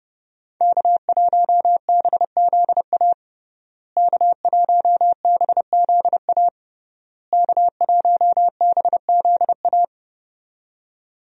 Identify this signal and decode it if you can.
Morse code, heard as a keyed tone, about 30 wpm, 700 Hz. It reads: K1BZA K1BZA K1BZA
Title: USA Call Signs - 5 Characters - Repeated 3x - 30wpm